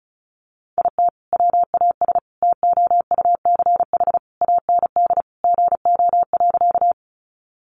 Morse code, heard as a keyed tone, about 34 wpm, 700 Hz. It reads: IT WAS TOUCH AND GO.
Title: Sentences from Top 1200 Words 35wpm